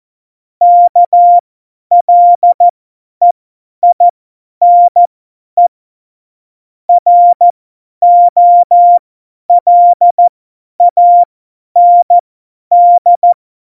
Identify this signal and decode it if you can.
Morse code, heard as a keyed tone, about 14 wpm, spaced as for 10 wpm, 700 Hz. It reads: KLEINE ROLAND
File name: Morsekurs / Witze / Witz 17. Oktober 17, Tempo 50